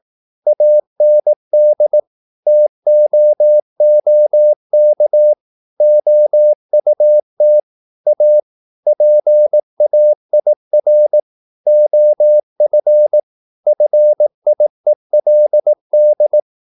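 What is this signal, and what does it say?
Morse code, heard as a keyed tone, about 18 wpm, 600 Hz. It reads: AND TOOK OUT A PAIR OF FIELD
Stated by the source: Morse practice at 18 wpm, English text, from CW med Gnister.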